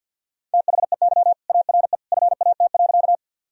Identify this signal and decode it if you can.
Morse code, heard as a keyed tone, about 50 wpm, 700 Hz. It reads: THEY ARE FAT.
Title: Sentences from Top 1000 Words - Review Part 4 - 50wpm